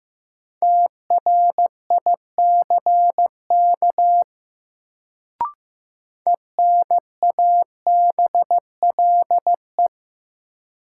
Morse code, heard as a keyed tone, about 15 wpm, 700 Hz. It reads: TRICK ENABLE
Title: Sentences from Top 1500 Words - New Word Drill - 15wpm